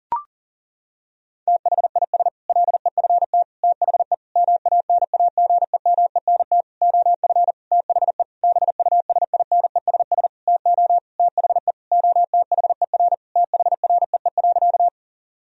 Morse code, heard as a keyed tone, about 40 wpm, 700 Hz. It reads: THIS LEFT THE MANAGEMENT OF THE BUSINESS TO THE OTHER THREE.
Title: Sentences from Top 1000 Words - Review Part 4 - 40wpm